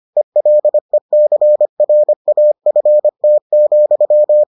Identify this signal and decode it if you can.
Morse code, heard as a keyed tone, about 25 wpm, 600 Hz. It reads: ELECRAFT,